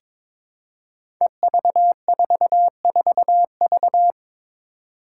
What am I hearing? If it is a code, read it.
Morse code, heard as a keyed tone, about 22 wpm, 700 Hz. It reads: EV44V
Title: Five Challenging Letter-Number Combinations 22wpm